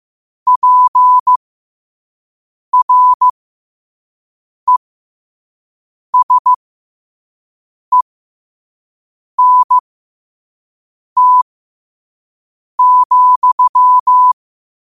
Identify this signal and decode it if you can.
Morse code, heard as a keyed tone, about 15 wpm, spaced as for 5 wpm, 1000 Hz. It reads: PRESENT,
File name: qotdfiles / 5wpm